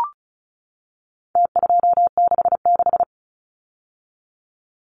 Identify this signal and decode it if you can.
Morse code, heard as a keyed tone, about 34 wpm, 700 Hz. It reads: T266
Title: Numbers - Contest Style - 4-Digit 35wpm